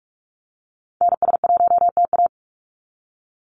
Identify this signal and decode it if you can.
Morse code, heard as a keyed tone, about 44 wpm, 700 Hz. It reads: NS1TA
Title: NA SOTA Activator Callsigns 45wpm